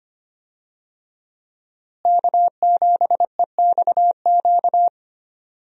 Morse code, heard as a keyed tone, about 25 wpm, 700 Hz. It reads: K7EXQ
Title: USA Call Signs - 5 Characters - Repeated 3x - 25wpm